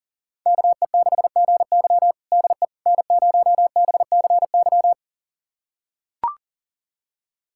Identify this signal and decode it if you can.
Morse code, heard as a keyed tone, about 40 wpm, 700 Hz. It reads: KE6GY DE N0BCY
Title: USA Callsign DE Callsign 40wpm